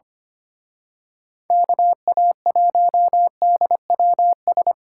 Morse code, heard as a keyed tone, about 29 wpm, 700 Hz. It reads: KA1DWS
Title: Speed Racing - USA Call Signs 25wpm